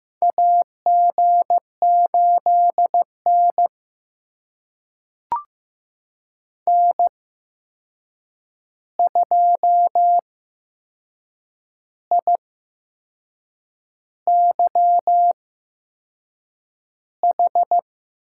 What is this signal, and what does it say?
Morse code, heard as a keyed tone, about 15 wpm, spaced as for 6 wpm, 700 Hz. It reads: AG8N N2IYH
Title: USA Call Signs - 8x Character Spacing - 15wpm